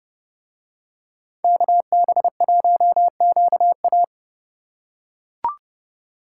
Morse code, heard as a keyed tone, about 31 wpm, 700 Hz. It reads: KB1QA